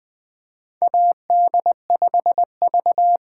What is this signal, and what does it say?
Morse code, heard as a keyed tone, about 20 wpm, 700 Hz. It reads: AD5V
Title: USA Call Signs - 8x Character Spacing - 20wpm